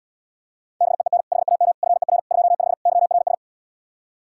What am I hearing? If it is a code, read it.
Morse code, heard as a keyed tone, about 79 wpm, 700 Hz. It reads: BEEN HAM FER 25 YRS